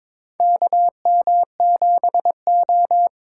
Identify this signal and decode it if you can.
Morse code, heard as a keyed tone, about 22 wpm, 700 Hz. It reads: KM7O
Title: USA Call Signs - 4 Characters - 2x Character Spacing - 22wpm